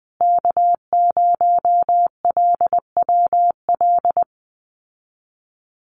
Morse code, heard as a keyed tone, about 20 wpm, 700 Hz. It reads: K0LWL
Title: USA Call Signs Repeated 3x 20wpm